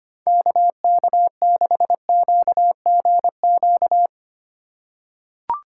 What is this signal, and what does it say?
Morse code, heard as a keyed tone, about 25 wpm, 700 Hz. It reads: KK6QGQ
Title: USA Call Signs - 6 Characters - 2x Character Spacing - 25wpm